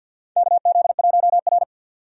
Morse code, heard as a keyed tone, about 52 wpm, 700 Hz. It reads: KZ1R
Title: USA Call Signs - 4 Characters - 50wpm